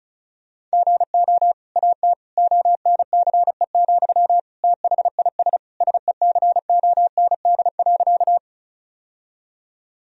Morse code, heard as a keyed tone, about 36 wpm, 700 Hz. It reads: GO AT ONCE, THIS SECOND.
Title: Sentences from Top 1000 Words - Review Part 1 - 35wpm